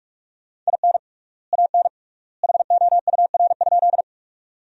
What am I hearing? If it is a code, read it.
Morse code, heard as a keyed tone, about 48 wpm, 700 Hz. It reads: IN AN HOUR?
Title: 3x Word-Spacing - Sentences from Top 400 Words - 45wpm